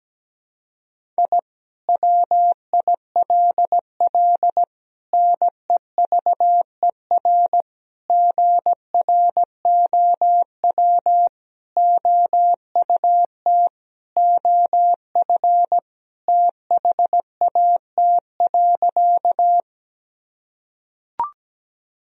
Morse code, heard as a keyed tone, about 17 wpm, 700 Hz. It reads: I WILL NEVER GROW OUT OF THAT.
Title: Sentences from Top 1000 Words - Review Part 1 - 17wpm